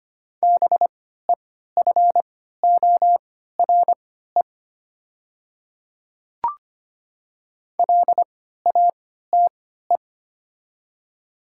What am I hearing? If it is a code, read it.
Morse code, heard as a keyed tone, about 25 wpm, spaced as for 14 wpm, 700 Hz. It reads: BEFORE LATE
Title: 3x Character Spacing - Top 300 Words - 25wpm